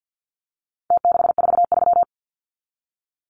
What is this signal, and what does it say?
Morse code, heard as a keyed tone, about 49 wpm, 700 Hz. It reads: T643
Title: Numbers - Contest Style - 4-Digit 50wpm